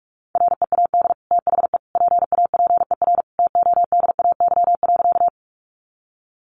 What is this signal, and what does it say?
Morse code, heard as a keyed tone, about 44 wpm, 700 Hz. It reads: READ THE PAPER TODAY.
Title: Sentences from Top 600 Words 45wpm